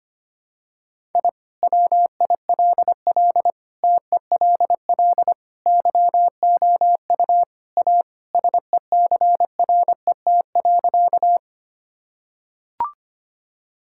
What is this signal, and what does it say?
Morse code, heard as a keyed tone, about 25 wpm, 700 Hz. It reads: I WILL TELL YOU A SECRET.
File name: practice / 25wpm / Sentences with 2-Word Phrases - 25wpm